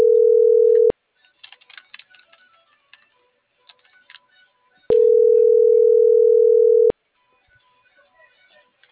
Ringback tone (North America), 440 + 480 Hz.